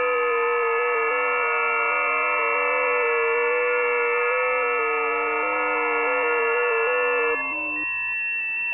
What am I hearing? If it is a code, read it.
FT8.
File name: tests / websdr test6